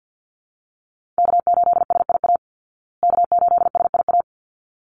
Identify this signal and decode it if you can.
Morse code, heard as a keyed tone, about 50 wpm, 700 Hz. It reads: K8SIA K8SIA
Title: Call Signs Repeated 3x - 50wpm